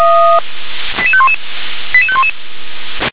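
Five-tone selective call (ZVEI), heard as quick runs of five tones, tones of about 80 ms. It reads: 88511 88511
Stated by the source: SDR-received ZVEI five-tone call 88743.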